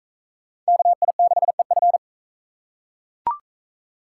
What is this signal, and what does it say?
Morse code, heard as a keyed tone, about 38 wpm, 700 Hz. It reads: KI6EF